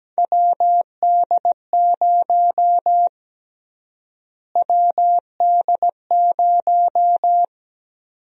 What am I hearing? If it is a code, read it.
Morse code, heard as a keyed tone, about 17 wpm, 700 Hz. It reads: WD0 WD0
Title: USA Call Sign Prefixes - Repeated 2x - 17wpm